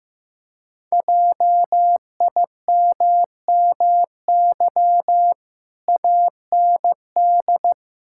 Morse code, heard as a keyed tone, about 14 wpm, 700 Hz. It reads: JIMMY AND
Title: Pangram Challenge v2 - 15wpm